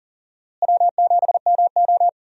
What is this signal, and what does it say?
Morse code, heard as a keyed tone, about 40 wpm, 700 Hz. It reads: W7MO
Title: USA Call Signs - 4 Characters - Repeated 3x - 40wpm